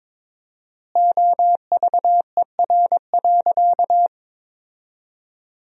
Morse code, heard as a keyed tone, about 21 wpm, 700 Hz. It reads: OVER.
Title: 4x Word-Spacing - Sentences from Top 300 Words - 22wpm